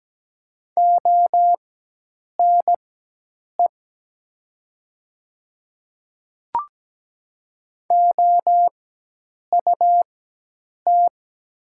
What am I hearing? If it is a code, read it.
Morse code, heard as a keyed tone, about 17 wpm, spaced as for 8 wpm, 700 Hz. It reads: ONE OUT
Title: 4x Character Spacing - Top 100 Words - 17wpm